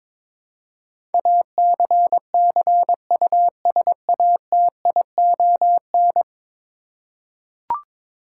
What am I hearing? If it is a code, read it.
Morse code, heard as a keyed tone, about 22 wpm, 700 Hz. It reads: ACCUSATION
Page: morse-code-ninja-practice-22wpm